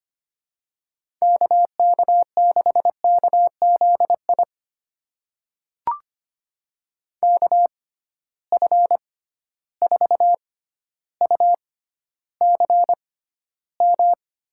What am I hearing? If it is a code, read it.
Morse code, heard as a keyed tone, about 25 wpm, spaced as for 13 wpm, 700 Hz. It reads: KK6KZI KF4UCM
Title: USA Call Signs - 6x Character Spacing - 25wpm